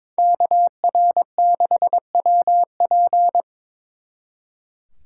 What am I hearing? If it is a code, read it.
Morse code, heard as a keyed tone, about 26 wpm, 700 Hz. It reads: KR6WP
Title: Speed Racing - USA 2x2 Call Signs 22wpm